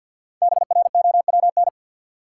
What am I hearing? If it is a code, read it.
Morse code, heard as a keyed tone, about 53 wpm, 700 Hz. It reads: BROWN
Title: Sentences from Top 1200 Words - New Word Drill - 50wpm